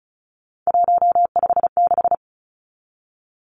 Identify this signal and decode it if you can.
Morse code, heard as a keyed tone, about 34 wpm, 700 Hz. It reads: 156
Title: Numbers - 3-Digit 35wpm